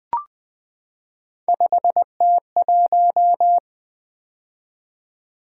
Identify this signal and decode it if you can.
Morse code, heard as a keyed tone, about 20 wpm, 700 Hz. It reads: 5T1
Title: Three Letter-Number Combinations - 20wpm